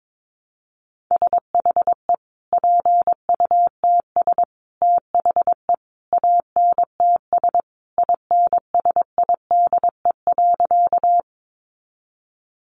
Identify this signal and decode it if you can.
Morse code, heard as a keyed tone, about 22 wpm, 700 Hz. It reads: SHE PUTS THE ANTS INSIDE.
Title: Additional Easy Phrases 22wpm